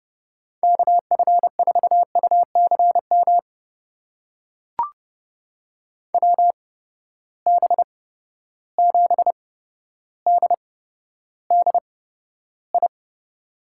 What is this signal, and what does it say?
Morse code, heard as a keyed tone, about 31 wpm, spaced as for 13 wpm, 700 Hz. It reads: KF4UCM WB7DDI
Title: USA Call Signs - 6 Characters - 8x Character Spacing - 30wpm